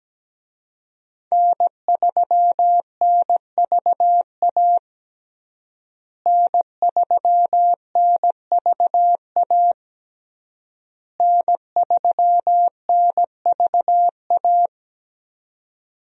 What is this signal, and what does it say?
Morse code, heard as a keyed tone, about 17 wpm, 700 Hz. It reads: N3NVA N3NVA N3NVA
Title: USA Call Signs - 5 Characters - Repeated 3x - 17wpm